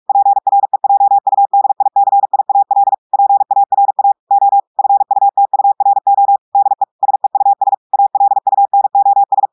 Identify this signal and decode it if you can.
Morse code, heard as a keyed tone, about 46 wpm, 800 Hz. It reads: PREJUDICIAL PARA O FUTURO DE SEUS ALUNOS
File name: portugais texte en clair 1 vitesse 45 mots minutes